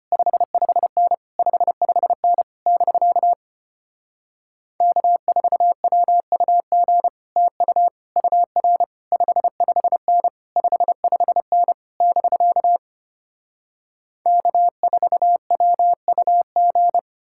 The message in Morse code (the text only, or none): 55N 55N <BK> K4WUG TU UR 55N 55N <BK> K4WUG